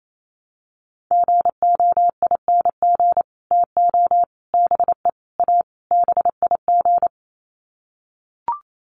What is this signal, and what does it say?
Morse code, heard as a keyed tone, about 28 wpm, 700 Hz. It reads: GOING TO BE A BIG
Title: Sets of 5 Words Encore2 28wpm